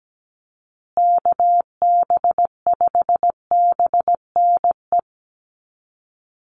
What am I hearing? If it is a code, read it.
Morse code, heard as a keyed tone, about 17 wpm, 700 Hz. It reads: KB5BNE
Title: USA Call Signs Repeated 3x 17wpm